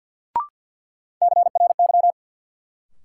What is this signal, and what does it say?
Morse code, heard as a keyed tone, about 47 wpm, 700 Hz. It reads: CRY
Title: Sets of 5 Words Encore - New Word Drill - 50wpm